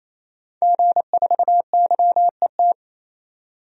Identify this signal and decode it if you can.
Morse code, heard as a keyed tone, about 27 wpm, 700 Hz. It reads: G4YET
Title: DXCC-100 Call Signs - 5 Characters - 28wpm